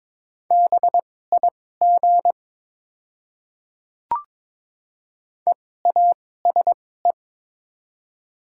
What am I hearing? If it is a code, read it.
Morse code, heard as a keyed tone, about 22 wpm, spaced as for 16 wpm, 700 Hz. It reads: BIG EASE
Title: 2x Character Spacing - Top 300 Words - 22wpm